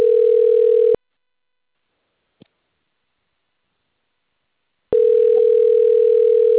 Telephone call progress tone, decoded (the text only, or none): ringback tone